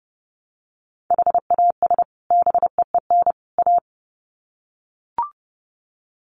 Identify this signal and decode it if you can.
Morse code, heard as a keyed tone, about 30 wpm, 700 Hz. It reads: HAS BEEN A
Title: Sets of 3 Words 30wpm